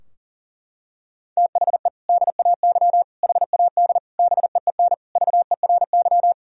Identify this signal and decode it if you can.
Morse code, heard as a keyed tone, about 40 wpm, 700 Hz. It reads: THE DAY HAD BEEN VERY
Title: Sentences from Top 1000 Words - Review Part 3 - 40wpm